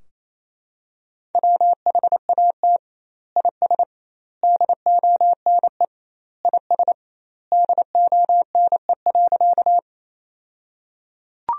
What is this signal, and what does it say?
Morse code, heard as a keyed tone, about 28 wpm, 700 Hz. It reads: WHAT IS DONE IS DONE.
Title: 2x Word-Spacing - Sentences from Top 500 Words - 28wpm